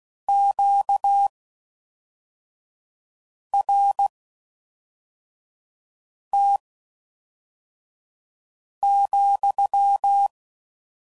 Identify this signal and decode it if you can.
Morse code, heard as a keyed tone, about 16 wpm, spaced as for 4 wpm, 800 Hz. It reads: QRT,